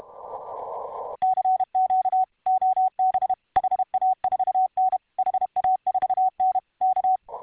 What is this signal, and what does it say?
Morse code, heard as a keyed tone, about 31 wpm, 750 Hz. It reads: CQ OB HA4N HA4N K